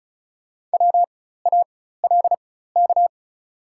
Morse code, heard as a keyed tone, about 36 wpm, spaced as for 16 wpm, 690 Hz. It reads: WALK
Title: 4x Character Spacing - Top 300 Words - 35wpm